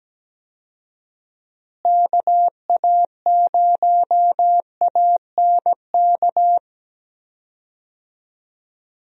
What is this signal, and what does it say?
Morse code, heard as a keyed tone, about 17 wpm, 700 Hz. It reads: KA0ANK
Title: USA Call Signs - 6 Characters - Repeated 3x - 17wpm